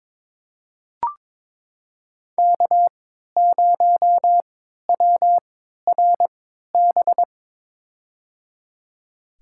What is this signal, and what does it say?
Morse code, heard as a keyed tone, about 22 wpm, spaced as for 13 wpm, 700 Hz. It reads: K0WRB
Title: USA Call Signs - 5 Characters - 3x Character Spacing - 22wpm